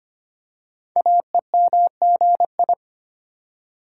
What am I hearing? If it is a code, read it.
Morse code, heard as a keyed tone, about 25 wpm, 700 Hz. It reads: AEMGI
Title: Code Groups - 5 Letters - 25wpm